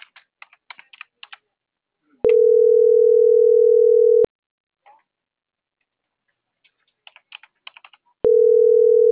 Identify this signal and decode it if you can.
Ringback tone (North America), 440 + 480 Hz.